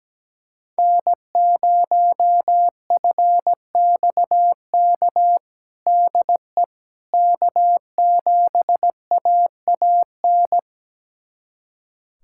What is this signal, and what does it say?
Morse code, heard as a keyed tone, about 17 wpm, 700 Hz. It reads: N0FXK DE K7AAN
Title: USA Callsign DE Callsign - 5 Characters - 17wpm